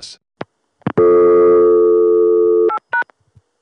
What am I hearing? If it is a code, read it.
Dial tone (North America), 350 + 440 Hz.